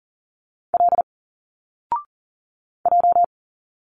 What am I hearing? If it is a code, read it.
Morse code, heard as a keyed tone, about 40 wpm, 700 Hz. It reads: L J
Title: Single Letters 40wpm